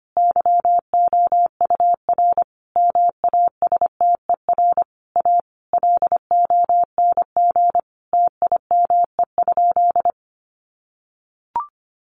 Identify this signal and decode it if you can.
Morse code, heard as a keyed tone, about 25 wpm, 700 Hz. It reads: YOUR MASTER A LONG TIME?
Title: Sentences from Top 1300 Words 25wpm